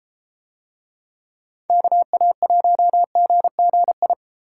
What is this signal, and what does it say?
Morse code, heard as a keyed tone, about 30 wpm, 700 Hz. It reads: KA1GGI